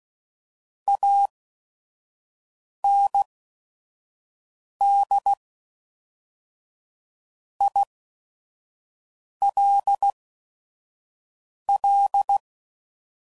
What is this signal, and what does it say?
Morse code, heard as a keyed tone, about 16 wpm, spaced as for 5 wpm, 800 Hz. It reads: AND ILL
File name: CW Code Groups @5WPM / QSOs / kc7haa